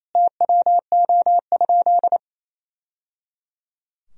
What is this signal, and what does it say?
Morse code, heard as a keyed tone, about 27 wpm, 700 Hz. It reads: TWO?
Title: Sentences from Top 1000 Words - Review Part 2 - 28wpm